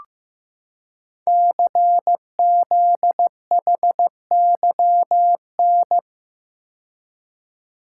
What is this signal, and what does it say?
Morse code, heard as a keyed tone, about 15 wpm, 700 Hz. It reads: CZHYN
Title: Code Groups - 5 Letters - 15wpm